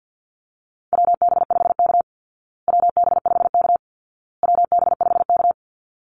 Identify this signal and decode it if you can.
Morse code, heard as a keyed tone, about 50 wpm, 700 Hz. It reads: WB5K WB5K WB5K
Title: Call Signs Repeated 2x - 50wpm